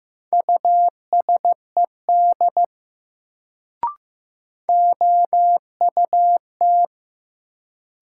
Morse code, heard as a keyed tone, about 16 wpm, 700 Hz. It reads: USED OUT
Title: Sets of 2 Words - New Word Drill - 15wpm